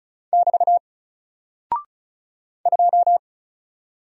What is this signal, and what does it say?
Morse code, heard as a keyed tone, about 36 wpm, 700 Hz. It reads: <BT> 2
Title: Single Character (Letter:Number:Punctuation:Prosign) Alt 35wpm